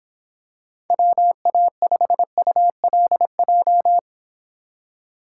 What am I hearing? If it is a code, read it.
Morse code, heard as a keyed tone, about 26 wpm, 700 Hz. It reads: WA5ULJ